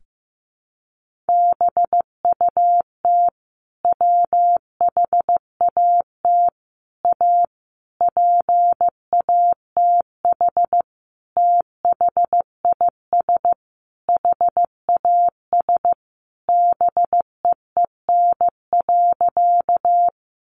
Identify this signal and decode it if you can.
Morse code, heard as a keyed tone, about 15 wpm, 700 Hz. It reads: BUT WHAT A PATH THIS HAS BEEN.
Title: Sentences from Top 1400 Words 15wpm